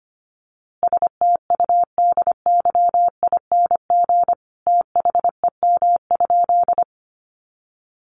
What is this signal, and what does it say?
Morse code, heard as a keyed tone, about 25 wpm, 700 Hz. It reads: STUDYING THEM?
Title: A Sound Of Thunder 25wpm